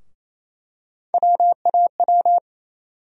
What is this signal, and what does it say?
Morse code, heard as a keyed tone, about 28 wpm, 700 Hz. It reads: WAW
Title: USA Call Sign Postfixes - 3 Letters 28wpm